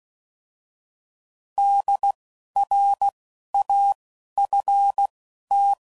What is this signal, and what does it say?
Morse code, heard as a keyed tone, about 16 wpm, spaced as for 11 wpm, 800 Hz. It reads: DRAFT
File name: Prefixes / By First Letter / Ds